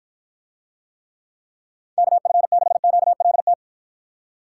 Morse code, heard as a keyed tone, about 53 wpm, 700 Hz. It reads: KF6QLT